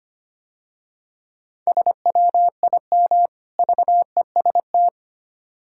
Morse code, heard as a keyed tone, about 30 wpm, 700 Hz. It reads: SWIM VEST
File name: practice / 25wpm / Difficult Words - Speed-Racing with Hints - 25wpm